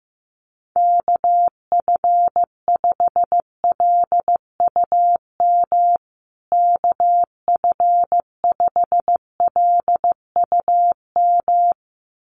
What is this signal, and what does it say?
Morse code, heard as a keyed tone, about 15 wpm, 700 Hz. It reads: KF5LUM KF5LUM
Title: USA Call Signs Repeated 2x 15wpm